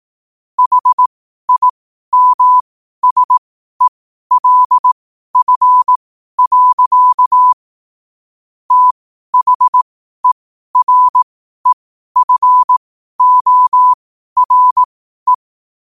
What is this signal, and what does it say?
Morse code, heard as a keyed tone, about 18 wpm, spaced as for 12 wpm, 1000 Hz. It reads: HIMSELF. THEREFORE
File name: qotdfiles / 12wpm